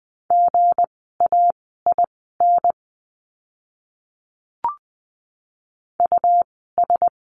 Morse code, heard as a keyed tone, about 20 wpm, spaced as for 15 wpm, 700 Hz. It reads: GAIN US
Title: 160 Lesson 26f - Letter g - Words 20wpm